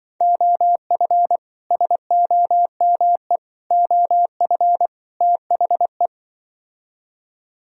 Morse code, heard as a keyed tone, about 25 wpm, 700 Hz. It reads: OF SOME OF THE